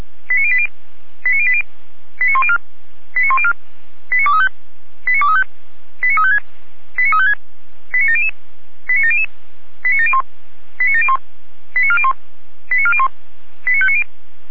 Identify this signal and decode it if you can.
Five-tone selective call (ZVEI), heard as quick runs of five tones, tones of about 70 ms. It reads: 89080 89080 89104 89104 89235 89235 89356 89356 89700 89700 89701 89701 89401 89401 89590